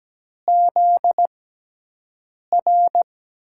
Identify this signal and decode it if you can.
Morse code, heard as a keyed tone, about 17 wpm, spaced as for 6 wpm, 700 Hz. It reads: ZR
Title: USA Call Signs - 5 Characters - 6x Character Spacing - 17wpm